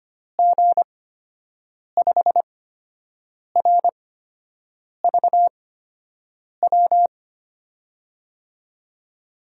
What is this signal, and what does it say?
Morse code, heard as a keyed tone, about 25 wpm, spaced as for 7 wpm, 700 Hz. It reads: G5RVW
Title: USA Call Signs - 6 Characters - 8x Character Spacing - 25wpm